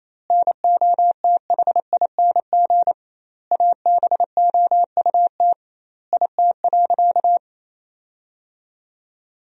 Morse code, heard as a keyed tone, about 28 wpm, 700 Hz. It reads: NOTHING ABOUT IT.
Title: 2x Word-Spacing - Sentences from Top 500 Words - 28wpm